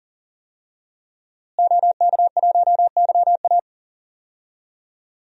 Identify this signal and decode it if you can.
Morse code, heard as a keyed tone, about 40 wpm, 700 Hz. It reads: OK1YA
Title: DXCC-100 Call Signs - 5 Characters - Repeated 2x - 40wpm